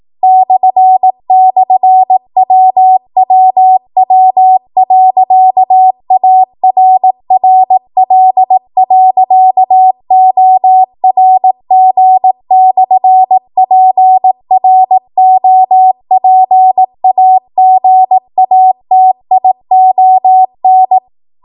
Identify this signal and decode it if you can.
Morse code, heard as a keyed tone, about 18 wpm, 750 Hz. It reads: //WWW.ARRL.ORG/PROPAGATION
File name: Archive / 18 WPM / 18WPM